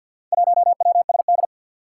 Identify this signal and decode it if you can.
Morse code, heard as a keyed tone, about 47 wpm, 700 Hz. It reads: 1WSD
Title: Four Challenging Letter-Number Combinations 50wpm